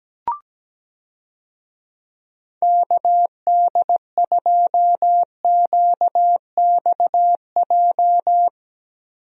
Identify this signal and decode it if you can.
Morse code, heard as a keyed tone, about 17 wpm, 700 Hz. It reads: KD2QXJ
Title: USA Call Signs - 6 Characters - Repeated 2x - 17wpm